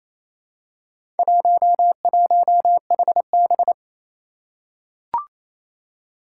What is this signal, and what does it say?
Morse code, heard as a keyed tone, about 28 wpm, 700 Hz. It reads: 11HB
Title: Four Challenging Letter-Number Combinations 28wpm